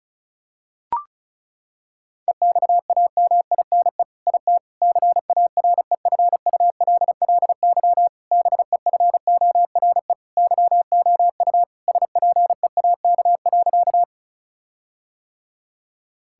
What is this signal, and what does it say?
Morse code, heard as a keyed tone, about 36 wpm, 700 Hz. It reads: EXAMINE IT CAREFULLY BEFORE YOU SPEAK.
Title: Sentences from Top 1600 Words 35wpm